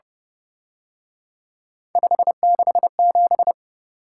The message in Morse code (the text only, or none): 567